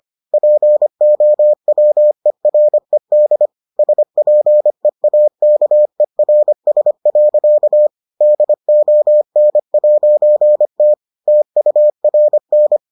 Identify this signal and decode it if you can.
Morse code, heard as a keyed tone, about 25 wpm, 600 Hz. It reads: POWERED SPEAKERS. DON'T TURN